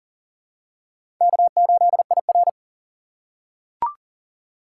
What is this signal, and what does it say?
Morse code, heard as a keyed tone, about 40 wpm, 690 Hz. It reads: K8IR
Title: USA Call Signs - 4 Characters - 2x Character Spacing - 40wpm